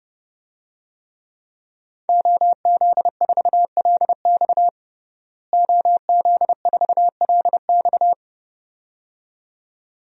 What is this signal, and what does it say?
Morse code, heard as a keyed tone, about 30 wpm, 700 Hz. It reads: OZ4LX OZ4LX
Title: DXCC-100 Call Signs - 5 Characters - Repeated 2x - 30wpm